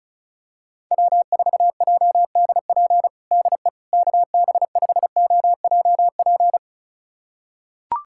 Morse code, heard as a keyed tone, about 34 wpm, 700 Hz. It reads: W4JDP DE KB5OJP
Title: USA Callsign DE Callsign 35wpm